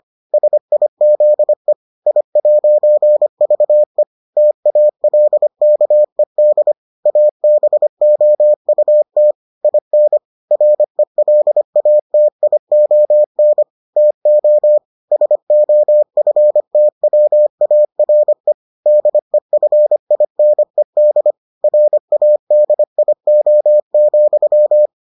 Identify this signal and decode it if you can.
Morse code, heard as a keyed tone, about 25 wpm, 600 Hz. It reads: SIZE I'VE TALKED ABOUT IN RELATION TO SOFTWARE DEFINED RADIO,